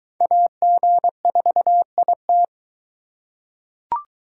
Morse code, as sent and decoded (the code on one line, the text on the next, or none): .- --. ....- .. -
AG4IT